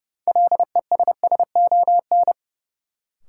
Morse code, heard as a keyed tone, about 30 wpm, 700 Hz. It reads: LESSON
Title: Sentences from Top 1200 Words - New Word Drill - 30wpm